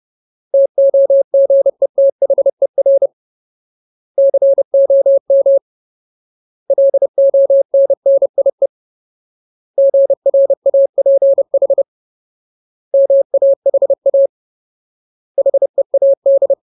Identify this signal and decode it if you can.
Morse code, heard as a keyed tone, about 30 wpm, 550 Hz. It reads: TOGETHER COM LONNIE GRAPH MAHA HEAD
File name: Ultimate Word List / 30WPM Ultimate